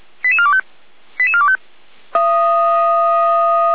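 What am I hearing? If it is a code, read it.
Five-tone selective call (ZVEI), heard as quick runs of five tones, tones of about 70 ms. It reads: 88425 88425
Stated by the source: ZVEI five-tone call received on SDR.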